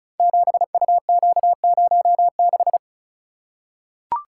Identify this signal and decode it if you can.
Morse code, heard as a keyed tone, about 36 wpm, 700 Hz. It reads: 7UQ06
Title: Code Groups - 5 Letter:Numbers - 35wpm